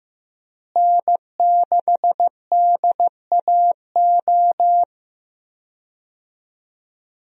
Morse code, heard as a keyed tone, about 15 wpm, 700 Hz. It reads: N6DAO
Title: USA Call Signs - 5 Characters - Repeated 2x - 15wpm